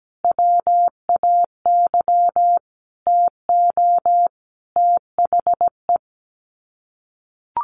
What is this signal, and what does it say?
Morse code, heard as a keyed tone, about 17 wpm, 700 Hz. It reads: WAY TO THE
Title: Sets of 5 Words Encore2 17wpm